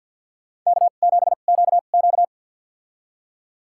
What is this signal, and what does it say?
Morse code, heard as a keyed tone, about 51 wpm, spaced as for 36 wpm, 700 Hz. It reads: K7QQ
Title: USA Call Signs - 4 Characters - 2x Character Spacing - 50wpm